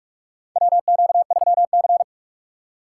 Morse code, heard as a keyed tone, about 44 wpm, 700 Hz. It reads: WQ3C
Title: USA Call Signs - 4 Characters - 4x Character Spacing - 45wpm